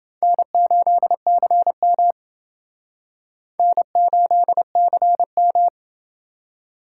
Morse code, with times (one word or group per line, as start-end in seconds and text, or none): 0.2-2.1 N8CM
3.6-5.7 N8CM